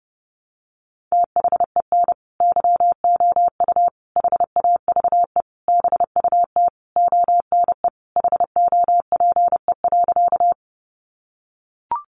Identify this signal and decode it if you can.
Morse code, heard as a keyed tone, about 30 wpm, 700 Hz. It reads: THEN YOU HAVE BUT ONE HOPE.
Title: Sentences from Top 1000 Words 30wpm